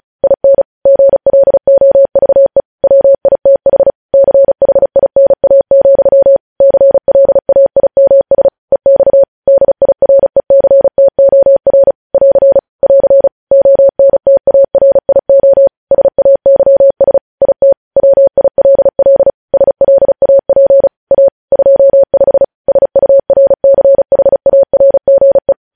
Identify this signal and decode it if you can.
Morse code, heard as a keyed tone, about 36 wpm, 550 Hz. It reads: IN GLOVE WITH CHINA, CLAIMS EX DIRECTOR <AR> <AR> ONTARIO SAYS IT WILL SLAP A 25 SURCHARGE
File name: podcast / News headlines-1x-35x35